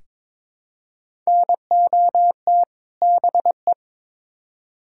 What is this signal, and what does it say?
Morse code, heard as a keyed tone, about 22 wpm, 700 Hz. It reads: NOT BE
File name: practice / 22wpm / Build A Phrase - 5-Word Phrases - 22wpm